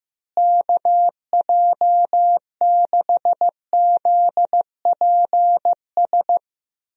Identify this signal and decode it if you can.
Morse code, heard as a keyed tone, about 15 wpm, 700 Hz. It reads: KJ6ZPS
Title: USA Call Signs - 6 Characters - Repeated 2x - 15wpm